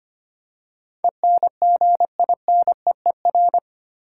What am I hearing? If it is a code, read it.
Morse code, heard as a keyed tone, about 25 wpm, 700 Hz. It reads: ENGINEER
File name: Eight Letter Words - 25wpm